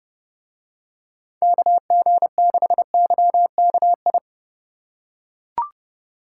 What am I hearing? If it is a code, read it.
Morse code, heard as a keyed tone, about 28 wpm, 700 Hz. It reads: KG6YKI